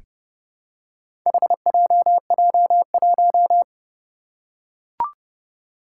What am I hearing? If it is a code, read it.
Morse code, heard as a keyed tone, about 31 wpm, 700 Hz. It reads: HJJ1
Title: Four Challenging Letter-Number Combinations 30wpm